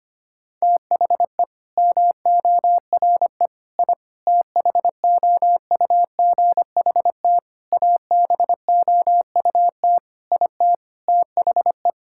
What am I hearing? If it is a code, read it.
Morse code, heard as a keyed tone, about 25 wpm, 700 Hz. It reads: THE MORE I THOUGHT ABOUT IT THE
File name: practice / 25wpm / Sentences with 9-Word Phrases - 25wpm